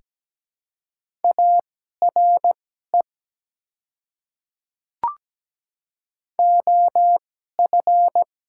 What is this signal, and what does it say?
Morse code, heard as a keyed tone, about 17 wpm, spaced as for 12 wpm, 690 Hz. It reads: ARE OF
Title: 2x Character Spacing - Top 100 Words - 17wpm